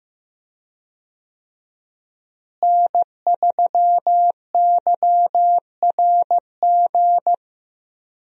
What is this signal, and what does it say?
Morse code, heard as a keyed tone, about 16 wpm, 700 Hz. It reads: N3YRG
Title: USA Call Signs - 5 Characters - Repeated 3x - 15wpm